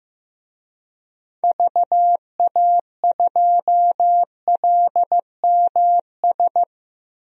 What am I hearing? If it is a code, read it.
Morse code, heard as a keyed tone, about 15 wpm, 700 Hz. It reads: VA2LMS
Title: DXCC-100 Call Signs - 6 Characters - Repeated 2x - 15wpm